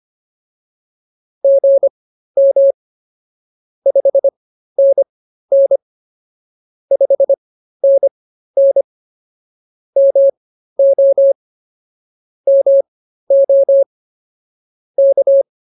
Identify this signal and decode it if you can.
Morse code, heard as a keyed tone, about 24 wpm, spaced as for 13 wpm, 550 Hz. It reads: GM 5NN 5NN MO MO K